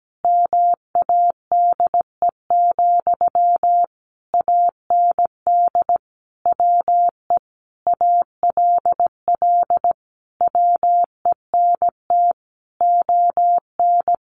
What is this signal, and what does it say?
Morse code, heard as a keyed tone, about 17 wpm, 700 Hz. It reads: MADE, AND WE ALL WENT ON
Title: Sentences from Top 1400 Words 17wpm